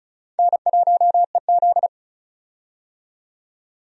Morse code, heard as a keyed tone, about 34 wpm, 700 Hz. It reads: N1EZ
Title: USA Call Signs - 4 Characters - 35wpm